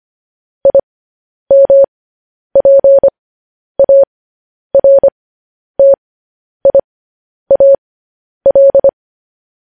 Morse code, heard as a keyed tone, about 25 wpm, spaced as for 10 wpm, 550 Hz. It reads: IMPARTIAL